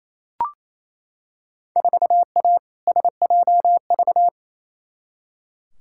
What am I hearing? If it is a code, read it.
Morse code, heard as a keyed tone, about 28 wpm, 700 Hz. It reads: 4A SJV
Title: Field Day Exchanges Encore - 28wpm